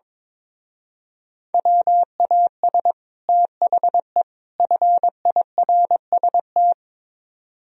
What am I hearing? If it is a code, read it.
Morse code, heard as a keyed tone, about 22 wpm, 700 Hz. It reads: WAS THE FIRST
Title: Speed-Racing - Sets of 3 Words 22wpm